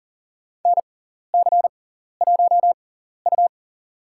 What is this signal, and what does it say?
Morse code, heard as a keyed tone, about 40 wpm, spaced as for 14 wpm, 700 Hz. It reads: NC1U
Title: USA Call Signs - 6x Character Spacing - 40wpm